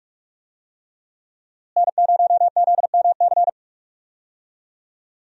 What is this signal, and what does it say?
Morse code, heard as a keyed tone, about 45 wpm, 700 Hz. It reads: N0ZMC
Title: USA Call Signs - 5 Characters - Repeated 3x - 45wpm